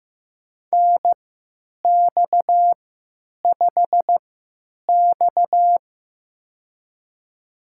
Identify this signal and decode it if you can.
Morse code, heard as a keyed tone, about 15 wpm, spaced as for 9 wpm, 700 Hz. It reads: NX5X
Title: USA Call Signs - 4 Characters - 3x Character Spacing - 15wpm